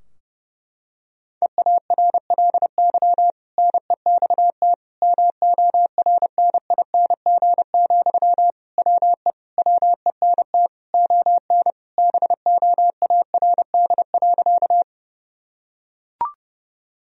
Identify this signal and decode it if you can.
Morse code, heard as a keyed tone, about 30 wpm, 700 Hz. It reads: EARLY NEXT MORNING, WE WENT ON BOARD.
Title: Sentences from Top 1000 Words - Review Part 4 - 30wpm